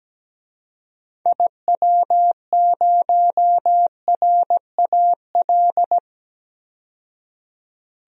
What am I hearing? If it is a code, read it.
Morse code, heard as a keyed tone, about 17 wpm, 700 Hz. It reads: IW0RAL
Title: DXCC-100 Call Signs - 6 Characters - 17wpm